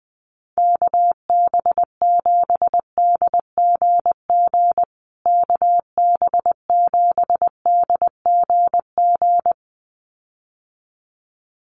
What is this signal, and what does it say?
Morse code, heard as a keyed tone, about 20 wpm, 700 Hz. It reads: KB7DGG KB7DGG
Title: USA Call Signs Repeated 2x 20wpm